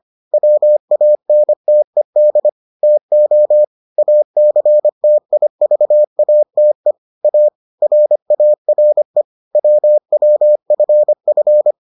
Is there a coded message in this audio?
Yes: Morse code from 0.3 to 11.7 s, heard as a keyed tone, about 25 wpm, 600 Hz. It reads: WANTED TO ACTIVATE A RARE WWFF